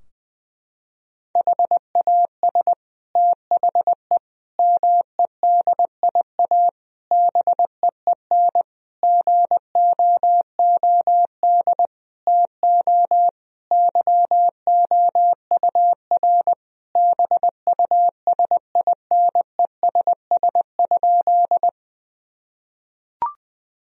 Morse code, heard as a keyed tone, about 20 wpm, 700 Hz. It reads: HAS THE MEDIA BEEN GOOD TO YOUR BUSINESS?
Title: Sentences from Top 1000 Words - Review Part 4 - 20wpm